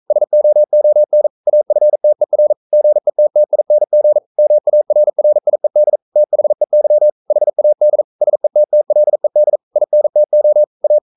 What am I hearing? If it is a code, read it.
Morse code, heard as a keyed tone, about 42 wpm, 600 Hz. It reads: SOON AFTER GETTING MARRIED THEY HAD SETTLED INTO A
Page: Romanen Never af Ken Follet på engelsk.